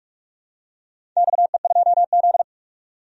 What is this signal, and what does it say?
Morse code, heard as a keyed tone, about 46 wpm, 700 Hz. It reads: XE2Z
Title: DXCC-100 Call Signs - 4 Characters - 45wpm